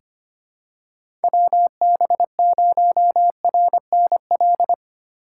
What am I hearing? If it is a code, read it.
Morse code, heard as a keyed tone, about 25 wpm, 700 Hz. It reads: WB0RNL
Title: USA Call Signs - 6 Characters - 2x Character Spacing - 25wpm